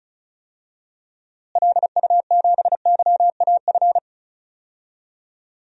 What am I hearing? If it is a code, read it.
Morse code, heard as a keyed tone, about 34 wpm, 700 Hz. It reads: LU7YAF
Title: DXCC-100 Call Signs - 6 Characters - Repeated 3x - 35wpm